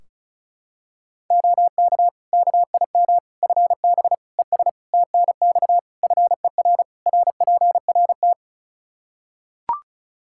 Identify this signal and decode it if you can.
Morse code, heard as a keyed tone, about 34 wpm, 700 Hz. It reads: OK KIM FB ES TNX FER RPRT
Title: Practice QSO - Exchange 2 - Info - 35wpm